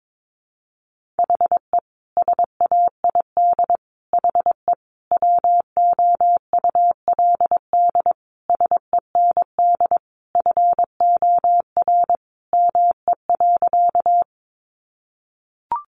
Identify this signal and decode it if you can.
Morse code, heard as a keyed tone, about 22 wpm, 700 Hz. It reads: HE SAID HE WOULD SEND FOR ME.
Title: Sentences from Top 1500 Words 22wpm